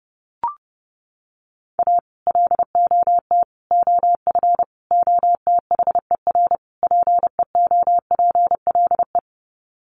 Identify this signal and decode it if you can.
Morse code, heard as a keyed tone, about 30 wpm, 700 Hz. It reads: A LOT OF OTHER PEOPLE
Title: Sets of 5 Words Encore2 30wpm